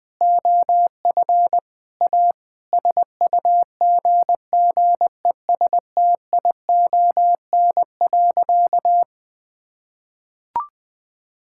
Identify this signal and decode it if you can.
Morse code, heard as a keyed tone, about 21 wpm, 700 Hz. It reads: OF A SUGGESTION.
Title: Sentences from Top 1600 Words 20wpm